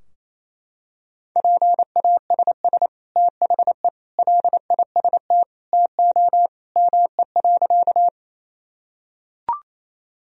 Morse code, heard as a keyed tone, about 28 wpm, 700 Hz. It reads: PASS THE LIST TO ME.
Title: Sentences from Top 1000 Words - Review Part 2 - 28wpm